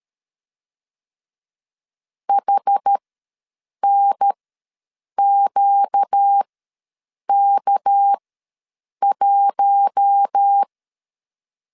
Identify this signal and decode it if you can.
Morse code, heard as a keyed tone, about 13 wpm, spaced as for 7 wpm, 800 Hz. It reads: HNQK1
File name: LEZIONI / 6 PKX0 / 6 PK ws 12wpm 3116